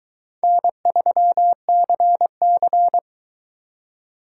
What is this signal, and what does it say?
Morse code, heard as a keyed tone, about 22 wpm, 700 Hz. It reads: N3CC